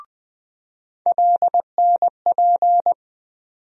Morse code, heard as a keyed tone, about 20 wpm, 700 Hz. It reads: LNP
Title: USA Call Sign Postfixes - 3 Letters 20wpm